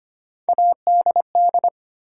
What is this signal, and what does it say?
Morse code, heard as a keyed tone, about 25 wpm, 700 Hz. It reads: ADD
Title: Sentences from Top 1000 Words - Review Part 1 - New Word Drill - 25wpm